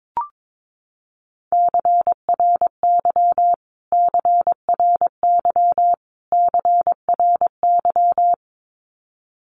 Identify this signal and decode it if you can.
Morse code, heard as a keyed tone, about 22 wpm, 700 Hz. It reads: CRY CRY CRY
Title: Top 500 Words Repeated 22wpm